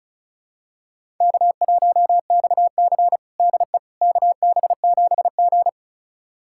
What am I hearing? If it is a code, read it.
Morse code, heard as a keyed tone, about 34 wpm, 700 Hz. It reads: K1XC DE KB7G
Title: USA Callsign DE Callsign - 4 Characters - 35wpm